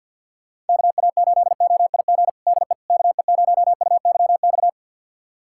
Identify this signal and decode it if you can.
Morse code, heard as a keyed tone, about 52 wpm, 700 Hz. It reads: KA8OIG DE KE0UYX
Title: USA Callsign DE Callsign - 6 Characters - 50wpm